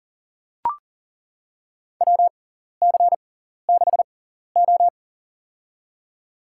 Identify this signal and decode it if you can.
Morse code, heard as a keyed tone, about 40 wpm, spaced as for 14 wpm, 700 Hz. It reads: WC6O
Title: USA Call Signs - 4 Characters - 6x Character Spacing - 40wpm